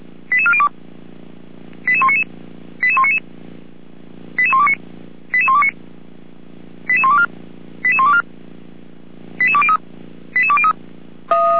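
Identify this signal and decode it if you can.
Five-tone selective call (ZVEI), heard as quick runs of five tones, tones of about 70 ms. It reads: 88402 88199 88199 88129 88129 88124 88124 88203 88203
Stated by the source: ZVEI five-tone selective call received by SDR.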